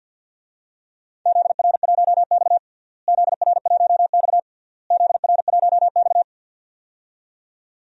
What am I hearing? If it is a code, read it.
Morse code, heard as a keyed tone, about 48 wpm, 700 Hz. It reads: ZR1X ZR1X ZR1X
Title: DXCC-100 Call Signs - 4 Characters - Repeated 3x - 50wpm